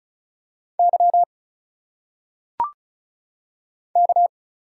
Morse code, heard as a keyed tone, about 34 wpm, 700 Hz. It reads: Y K
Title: Single Letters - Phonetic Pronunciation 35wpm